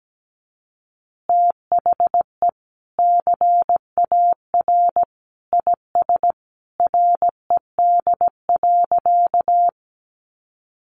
Morse code, heard as a keyed tone, about 17 wpm, 700 Hz. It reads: THE CAR IS RED.
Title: Easy Phrases 17wpm